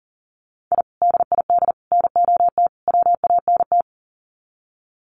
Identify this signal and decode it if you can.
Morse code, heard as a keyed tone, about 39 wpm, 700 Hz. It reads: I DID NOT WANT
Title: Sets of 4 Words 40wpm